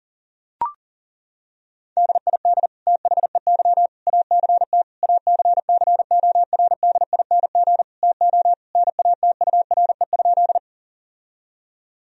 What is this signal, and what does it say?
Morse code, heard as a keyed tone, about 40 wpm, 700 Hz. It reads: DID THEY ACT ACCORDING TO NATURE?
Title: Sentences from Top 1000 Words - Review Part 4 - 40wpm